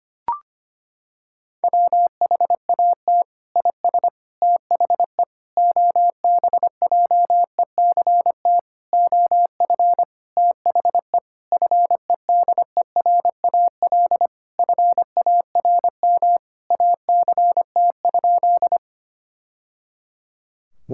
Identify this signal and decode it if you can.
Morse code, heard as a keyed tone, about 25 wpm, 700 Hz. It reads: WHAT IS THE OBJECT OF THE FEDERAL FARM ACT?
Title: Sentences from Top 1000 Words - Review Part 5 - 25wpm